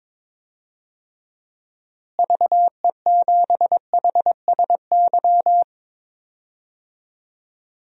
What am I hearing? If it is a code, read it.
Morse code, heard as a keyed tone, about 22 wpm, 700 Hz. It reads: VE7HSY
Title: DXCC-100 Call Signs - 6 Characters - Repeated 2x - 22wpm